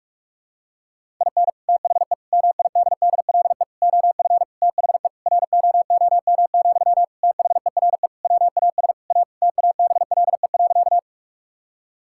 Morse code, heard as a keyed tone, about 46 wpm, 700 Hz. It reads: IN THE MIDDLE OF THE ROOM, THERE WAS A TABLE.
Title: Sentences from Top 1000 Words - Review Part 6 - 45wpm